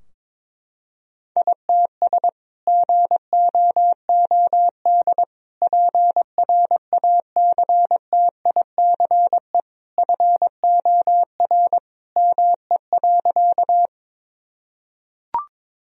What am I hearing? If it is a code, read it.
Morse code, heard as a keyed tone, about 22 wpm, 700 Hz. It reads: ITS GOOD PRACTICE FOR ME.
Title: Sentences from Top 1000 Words - Review Part 4 - 22wpm